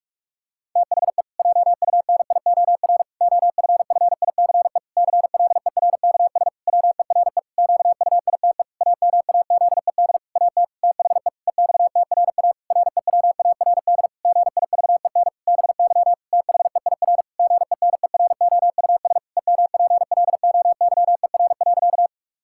Morse code, heard as a keyed tone, about 46 wpm, 700 Hz. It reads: THE JUNIOR OFFICE CLERKS WERE QUITE AMAZED AT THE EXTRA REWARD GIVEN BY THEIR GENEROUS EMPLOYER.
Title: Pangram Challenge v2 - 45wpm